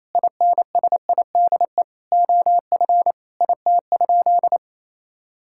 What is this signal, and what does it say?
Morse code, heard as a keyed tone, about 28 wpm, 700 Hz. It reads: INSIDE OF IT?
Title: Sentences from Top 1000 Words - Review Part 6 - 28wpm